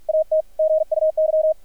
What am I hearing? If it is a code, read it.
Morse code, heard as a keyed tone, about 50 wpm, 630 Hz, with radio noise. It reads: AT GUY
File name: archived / CRSnet